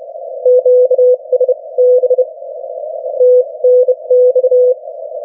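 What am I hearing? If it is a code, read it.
Morse code, heard as a keyed tone, about 30 wpm, spaced as for 23 wpm, 500 Hz. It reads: QSB TNX
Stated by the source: Morse from the New Zealand Net, the national amateur radio net of Aotearoa/New Zealand, meeting daily on 3535.0 kHz.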